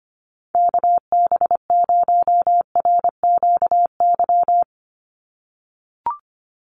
Morse code, heard as a keyed tone, about 25 wpm, 700 Hz. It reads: KB0RQY
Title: USA Call Signs 25wpm